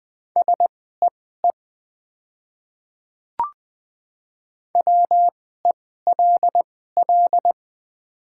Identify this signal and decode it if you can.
Morse code, heard as a keyed tone, about 20 wpm, spaced as for 15 wpm, 700 Hz. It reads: SEE WELL
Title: 2x Character Spacing - Top 200 Words - 20wpm